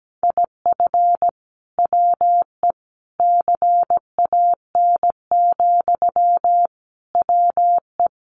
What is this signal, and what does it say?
Morse code, heard as a keyed tone, about 17 wpm, 700 Hz. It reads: IF WE CAN, WE
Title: Sentences from Top 1100 Words 17wpm